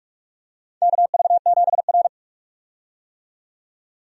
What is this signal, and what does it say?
Morse code, heard as a keyed tone, about 48 wpm, 700 Hz. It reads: KV7R
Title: USA Call Signs - 4 Characters - 45wpm